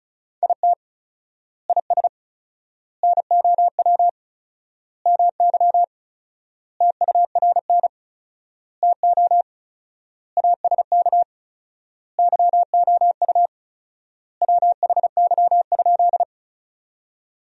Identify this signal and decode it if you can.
Morse code, heard as a keyed tone, about 34 wpm, 700 Hz. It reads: IT IS NOW MY TURN TO ASK YOU WHY?
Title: 4x Word-Spacing - Sentences from Top 200 Words - 35wpm